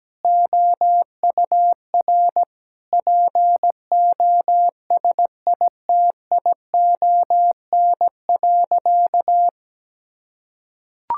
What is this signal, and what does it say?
Morse code, heard as a keyed tone, about 17 wpm, 700 Hz. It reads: OUR POSITION.
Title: Sentences from Top 1000 Words - Review Part 5 - 17wpm